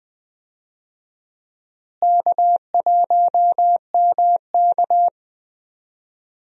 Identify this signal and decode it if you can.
Morse code, heard as a keyed tone, about 20 wpm, 700 Hz. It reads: K1MK
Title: USA Call Signs - 4 Characters - Repeated 3x - 20wpm